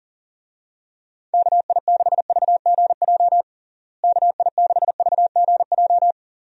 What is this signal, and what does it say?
Morse code, heard as a keyed tone, about 37 wpm, 700 Hz. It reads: KI6VGJ KI6VGJ
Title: USA Call Signs - 6 Characters - Repeated 2x - 40wpm